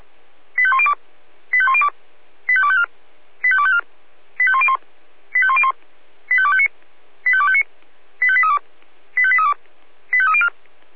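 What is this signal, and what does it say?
Five-tone selective call (ZVEI), heard as quick runs of five tones, tones of about 70 ms. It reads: 85102 85102 85254 85254 85101 85101 85259 85259 85832 85832 85304
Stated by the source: ZVEI five-tone selective call received on an SDR.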